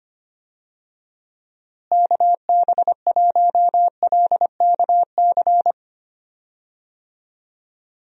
Morse code, heard as a keyed tone, about 25 wpm, 700 Hz. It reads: KB1LKC
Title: USA Call Signs - 6 Characters - Repeated 2x - 25wpm